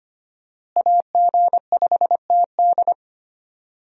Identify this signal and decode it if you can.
Morse code, heard as a keyed tone, about 25 wpm, 690 Hz. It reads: AG5TD
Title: USA Call Signs - 5 Characters - Repeated 2x - 25wpm